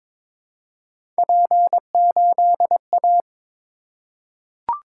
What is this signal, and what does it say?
Morse code, heard as a keyed tone, about 23 wpm, 700 Hz. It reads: P8A
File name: practice / 22wpm / Three Letter-Number Combinations - 22wpm